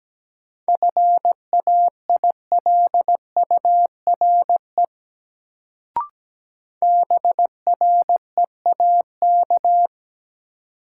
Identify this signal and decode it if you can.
Morse code, heard as a keyed tone, about 17 wpm, 700 Hz. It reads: FAILURE BREAK
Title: Sentences from Top 900 Words - New Word Drill - 17wpm